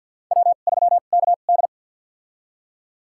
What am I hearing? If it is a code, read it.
Morse code, heard as a keyed tone, about 52 wpm, spaced as for 37 wpm, 700 Hz. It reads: W3KD
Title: USA Call Signs - 4 Characters - 2x Character Spacing - 50wpm